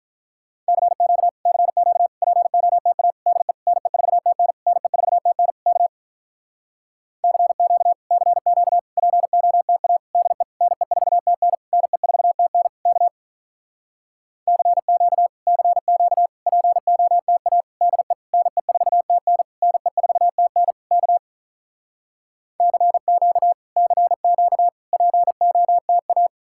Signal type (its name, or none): Morse code